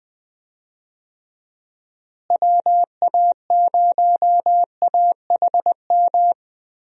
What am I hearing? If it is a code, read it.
Morse code, heard as a keyed tone, about 20 wpm, 700 Hz. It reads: WA0AHM
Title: USA Call Signs - 6 Characters - Repeated 3x - 20wpm